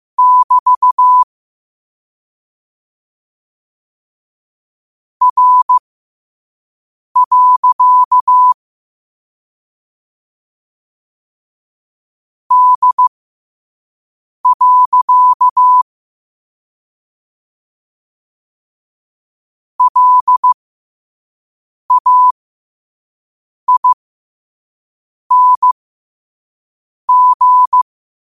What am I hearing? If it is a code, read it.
Morse code, heard as a keyed tone, about 15 wpm, spaced as for 5 wpm, 1000 Hz. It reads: <BT> R. D. LAING
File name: qotdfiles / 5wpm